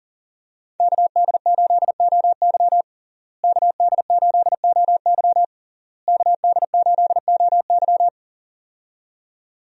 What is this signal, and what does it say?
Morse code, heard as a keyed tone, about 40 wpm, 690 Hz. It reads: KD8OY KD8OY KD8OY
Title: USA Call Signs - 5 Characters - Repeated 3x - 40wpm